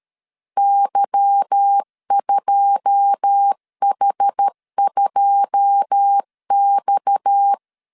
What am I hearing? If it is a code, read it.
Morse code, heard as a keyed tone, about 13 wpm, 800 Hz. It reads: Y2H2X
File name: LEZIONI / 9 2 12wpm 2301